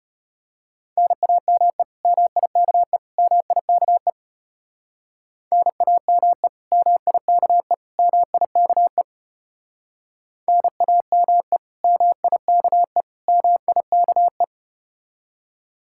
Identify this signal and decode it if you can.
Morse code, heard as a keyed tone, about 34 wpm, 700 Hz. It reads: NAME MIKE MIKE NAME MIKE MIKE NAME MIKE MIKE